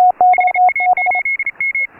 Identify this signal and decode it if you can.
Morse code, heard as a keyed tone, about 34 wpm, 2150 Hz. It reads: HA8M
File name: AUDIO2018 / HA3FMR CW